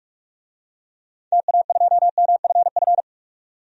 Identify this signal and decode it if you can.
Morse code, heard as a keyed tone, about 46 wpm, 700 Hz. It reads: TA2MFF
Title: DXCC-100 Call Signs - 6 Characters - 45wpm